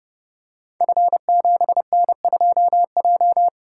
Morse code, heard as a keyed tone, about 30 wpm, 700 Hz. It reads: F7N2J
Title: Code Groups - 5 Letter:Numbers - 30wpm